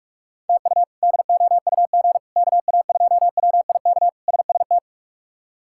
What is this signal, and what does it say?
Morse code, heard as a keyed tone, about 41 wpm, 700 Hz. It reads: TU DOUG KA2WIK SST
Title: K1USN SST QSO Practice - 45wpm